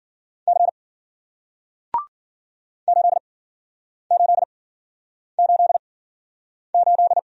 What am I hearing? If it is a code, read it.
Morse code, heard as a keyed tone, about 54 wpm, spaced as for 9 wpm, 700 Hz. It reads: C 8888